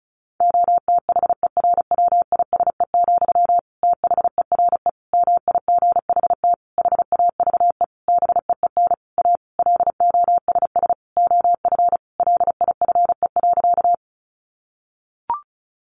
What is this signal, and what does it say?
Morse code, heard as a keyed tone, about 34 wpm, 700 Hz. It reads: OTHERWISE, THERE MIGHT HAVE BEEN A LOSS OF LIFE.
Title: Sentences from Top 1400 Words 35wpm